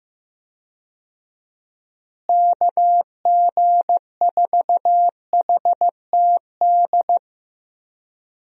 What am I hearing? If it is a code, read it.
Morse code, heard as a keyed tone, about 15 wpm, 700 Hz. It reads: KG4HTD
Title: USA Call Signs - 6 Characters - Repeated 2x - 15wpm